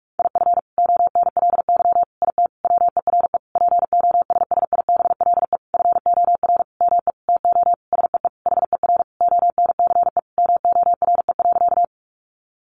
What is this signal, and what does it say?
Morse code, heard as a keyed tone, about 46 wpm, 700 Hz. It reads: IF ONLY IT WERE POSSIBLE FOR ME TO SEE HER ONCE MORE.
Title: Sentences from Top 500 Words 45wpm